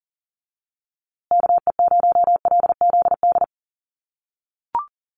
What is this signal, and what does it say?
Morse code, heard as a keyed tone, about 40 wpm, 700 Hz. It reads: KE0LZD
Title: USA Call Signs Repeated 2x 40wpm